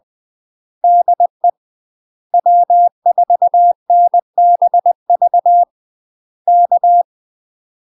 Morse code, heard as a keyed tone, about 20 wpm, 700 Hz. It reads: DE W4NBV K
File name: UK MASTER QSOs TRAINING / 20WPM / UK MASTER QSOs 0050